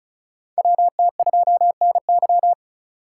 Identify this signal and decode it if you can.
Morse code, heard as a keyed tone, about 34 wpm, 700 Hz. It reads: WT2NY
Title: USA Call Signs - 3x Character Spacing - 35wpm